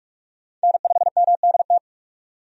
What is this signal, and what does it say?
Morse code, heard as a keyed tone, about 46 wpm, 700 Hz. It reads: N5MDT
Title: USA Call Signs - 5 Characters - 45wpm